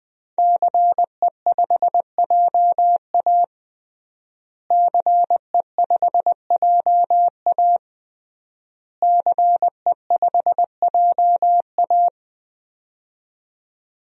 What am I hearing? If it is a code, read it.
Morse code, heard as a keyed tone, about 20 wpm, 700 Hz. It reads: CE5JA CE5JA CE5JA
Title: DXCC-100 Call Signs - 5 Characters - Repeated 3x - 20wpm